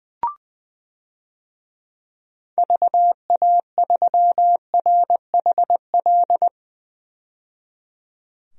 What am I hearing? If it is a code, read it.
Morse code, heard as a keyed tone, about 20 wpm, 700 Hz. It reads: VA3RHL
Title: DXCC-100 Call Signs - 6 Characters - 20wpm